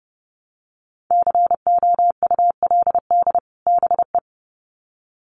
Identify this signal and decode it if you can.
Morse code, heard as a keyed tone, about 30 wpm, 700 Hz. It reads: COULD BE
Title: Sets of 2 Words 30wpm